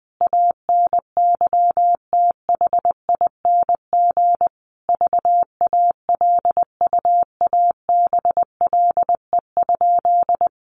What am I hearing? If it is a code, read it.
Morse code, heard as a keyed tone, about 20 wpm, 700 Hz. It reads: ANYTHING VALUABLE?
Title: Sentences from Top 1100 Words 20wpm